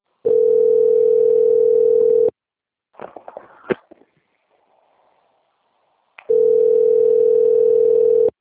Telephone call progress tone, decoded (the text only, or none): ringback tone